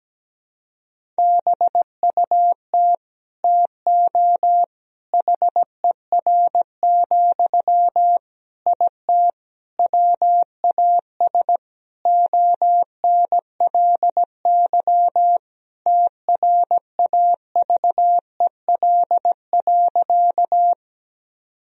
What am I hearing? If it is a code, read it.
Morse code, heard as a keyed tone, about 17 wpm, 700 Hz. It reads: BUT TO HER, IT WAS ONLY TRAVEL.
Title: Sentences from Top 1000 Words - Review Part 2 - 17wpm